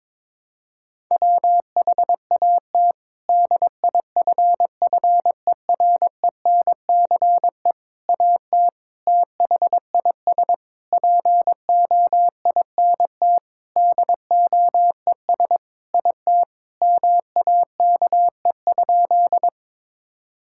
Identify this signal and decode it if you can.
Morse code, heard as a keyed tone, about 21 wpm, 700 Hz. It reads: WHAT DIFFERENCE AT THIS POINT DOES IT MAKE?
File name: practice / 22wpm / Sentences with 8-Word Phrases - 22wpm